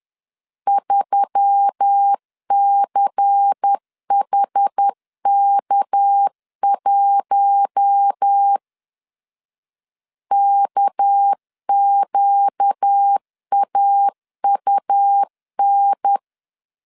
Morse code, heard as a keyed tone, about 10 wpm, 800 Hz. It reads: 3CHK1 KQAUN
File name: LEZIONI / 6 PKX0 / 6 PK 10wpm 2436